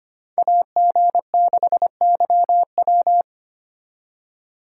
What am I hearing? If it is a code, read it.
Morse code, heard as a keyed tone, about 32 wpm, 700 Hz. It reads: AG6YW